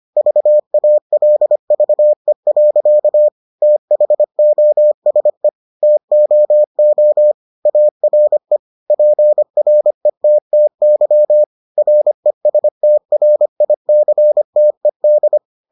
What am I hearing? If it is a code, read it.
Morse code, heard as a keyed tone, about 25 wpm, 600 Hz. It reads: VALVE. THOSE TOO ARE PRETTY RESTRICTED